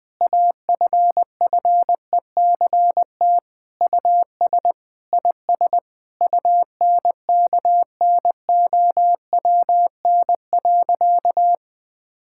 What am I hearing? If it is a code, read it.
Morse code, heard as a keyed tone, about 20 wpm, 700 Hz. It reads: AFFECT US IS UNKNOWN.
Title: Sentences with 4-Word Phrases - 20wpm